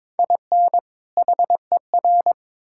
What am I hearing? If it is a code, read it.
Morse code, heard as a keyed tone, about 22 wpm, 700 Hz. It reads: IN HER
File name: Build A Phrase - 2-Word Phrases - 22wpm